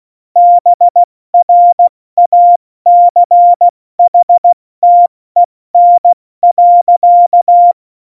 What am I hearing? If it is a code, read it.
Morse code, heard as a keyed tone, about 16 wpm, spaced as for 14 wpm, 700 Hz. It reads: BRACHTEN.
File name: Morsekurs / Witze / Witz 26. Januar 2021, 70 F80